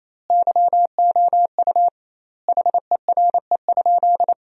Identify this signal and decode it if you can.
Morse code, heard as a keyed tone, about 28 wpm, 700 Hz. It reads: YOU HERE?
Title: 2x Word-Spacing - Sentences from Top 500 Words - 28wpm